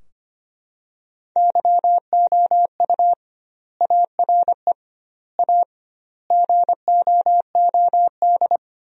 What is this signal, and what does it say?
Morse code, heard as a keyed tone, about 25 wpm, 700 Hz. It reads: YOU ARE A GOOD
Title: 2x Word-Spacing - Sentences from Top 400 Words - 25wpm